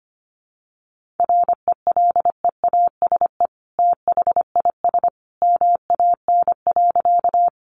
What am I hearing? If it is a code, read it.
Morse code, heard as a keyed tone, about 25 wpm, 700 Hz. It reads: RELEASE THIS MAN.
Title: Sentences from Top 1100 Words 25wpm